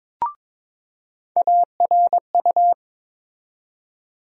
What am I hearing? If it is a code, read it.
Morse code, heard as a keyed tone, about 22 wpm, 700 Hz. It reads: ARU
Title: USA Call Sign Postfixes - 3 Letters 22wpm